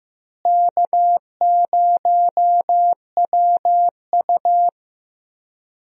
Morse code, heard as a keyed tone, about 15 wpm, 700 Hz. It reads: K0WU
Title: USA Call Signs - 4 Characters - Repeated 2x - 15wpm